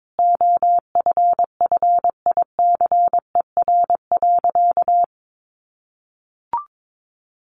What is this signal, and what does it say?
Morse code, heard as a keyed tone, about 22 wpm, 700 Hz. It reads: OFFICER.
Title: Sentences from Top 1200 Words 22wpm